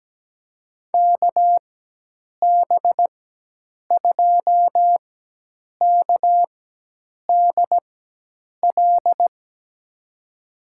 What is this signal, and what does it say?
Morse code, heard as a keyed tone, about 17 wpm, spaced as for 8 wpm, 700 Hz. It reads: KB2KDL
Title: USA Call Signs - 6 Characters - 4x Character Spacing - 17wpm